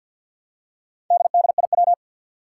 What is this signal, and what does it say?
Morse code, heard as a keyed tone, about 47 wpm, 700 Hz. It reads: DDIW
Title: Four Challenging Letter-Number Combinations 50wpm